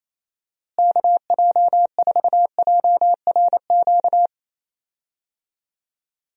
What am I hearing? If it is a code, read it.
Morse code, heard as a keyed tone, about 28 wpm, 700 Hz. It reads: KJ4JRQ
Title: USA Call Signs - 6 Characters - Repeated 2x - 28wpm